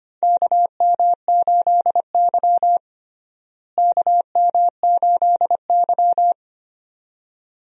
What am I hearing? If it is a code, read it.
Morse code, heard as a keyed tone, about 25 wpm, 700 Hz. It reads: KM8Y KM8Y
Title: USA Call Signs - 4 Characters - Repeated 2x - 25wpm